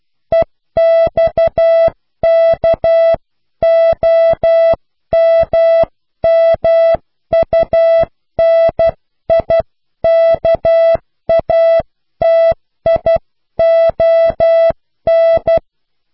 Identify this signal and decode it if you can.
Morse code, heard as a keyed tone, about 12 wpm, 660 Hz. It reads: EXKOMMUNIKATION
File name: cw / Pseudo woorden (15)